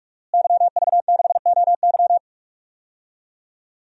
Morse code, heard as a keyed tone, about 44 wpm, 700 Hz. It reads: YV6OY
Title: DXCC-100 Call Signs - 5 Characters - Repeated 2x - 45wpm